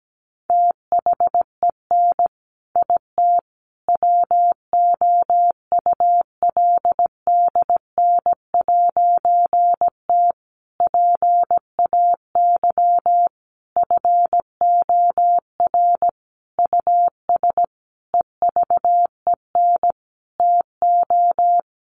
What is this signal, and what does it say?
Morse code, heard as a keyed tone, about 17 wpm, 700 Hz. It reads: THEN IT WOULDN'T PAY FOR US EVEN TO
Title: A Sound Of Thunder 17wpm